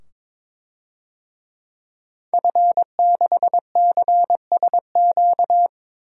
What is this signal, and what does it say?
Morse code, heard as a keyed tone, about 22 wpm, 700 Hz. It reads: F6CSQ
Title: DXCC-100 Call Signs - 5 Characters - 22wpm